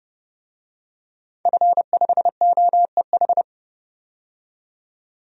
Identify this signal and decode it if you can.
Morse code, heard as a keyed tone, about 32 wpm, 700 Hz. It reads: F5OEH
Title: DXCC-100 Call Signs - 5 Characters - 30wpm